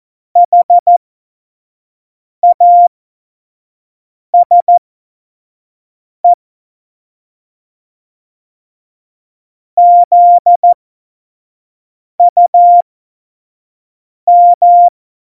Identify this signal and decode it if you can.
Morse code, heard as a keyed tone, about 14 wpm, spaced as for 5 wpm, 700 Hz. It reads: HASE ZUM
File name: Morsekurs / Witze / Witz 24. Oktober 17, Tempo 25